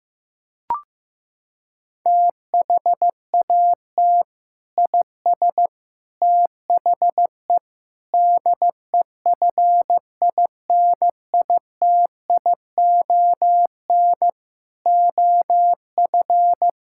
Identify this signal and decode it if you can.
Morse code, heard as a keyed tone, about 16 wpm, 700 Hz. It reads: THAT IS THE DEFINITION OF
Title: Sentences from Top 1000 Words - Review Part 6 - 15wpm